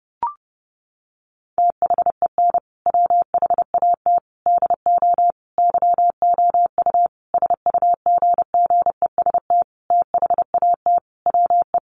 Morse code, heard as a keyed tone, about 30 wpm, 700 Hz. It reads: THEN WHAT DO YOU SUGGEST THAT WE
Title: Sentences from Top 1300 Words 30wpm